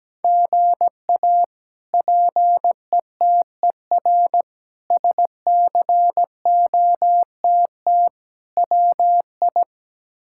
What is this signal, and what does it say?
Morse code, heard as a keyed tone, about 17 wpm, 700 Hz. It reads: GA PETER SCOTT WI
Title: K1USN SST QSO Practice - 17wpm